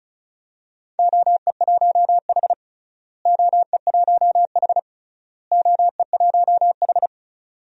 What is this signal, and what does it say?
Morse code, heard as a keyed tone, about 36 wpm, 700 Hz. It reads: OE1H OE1H OE1H
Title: DXCC-100 Call Signs - 4 Characters - Repeated 3x - 35wpm